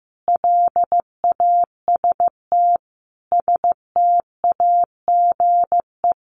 Morse code, heard as a keyed tone, about 15 wpm, 700 Hz. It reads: LAST STAGE
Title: Sentences from Top 1200 Words 15wpm